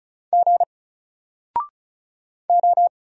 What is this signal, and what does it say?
Morse code, heard as a keyed tone, about 36 wpm, 700 Hz. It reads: G O
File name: Single-Letter-Number-35wpm